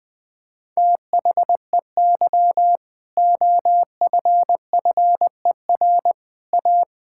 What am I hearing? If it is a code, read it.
Morse code, heard as a keyed tone, about 21 wpm, 700 Hz. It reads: THEY OFFER A
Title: Sentences with 4-Word Phrases - 20wpm